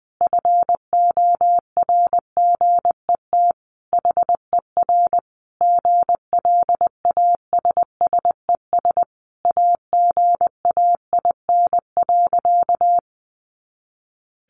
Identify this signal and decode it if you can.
Morse code, heard as a keyed tone, about 20 wpm, 700 Hz. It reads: FORGET HER GLASSES AGAIN.
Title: Additional Easy Phrases 20wpm